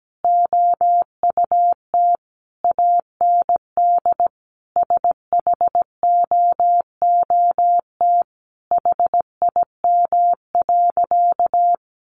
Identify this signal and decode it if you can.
Morse code, heard as a keyed tone, about 17 wpm, 700 Hz. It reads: OUT AND SHOOT HIM.
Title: Sentences from Top 1200 Words 17wpm